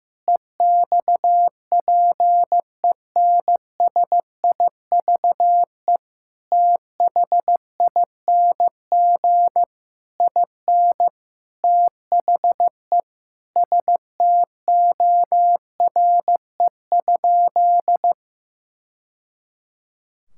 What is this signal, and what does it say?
Morse code, heard as a keyed tone, about 15 wpm, 700 Hz. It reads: EXPENSIVE THING IN THE STORE?
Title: Sentences from Top 1000 Words - Review Part 6 - 15wpm